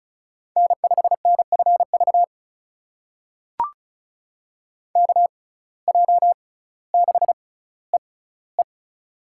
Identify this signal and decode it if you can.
Morse code, heard as a keyed tone, about 36 wpm, spaced as for 18 wpm, 700 Hz. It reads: N5NFV KJ6EE
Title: USA Call Signs - 5 Characters - 6x Character Spacing - 35wpm